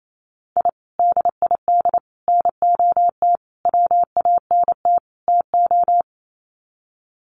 Morse code, heard as a keyed tone, about 28 wpm, 700 Hz. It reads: I DID NOT WANT TO
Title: Sets of 5 Words 28wpm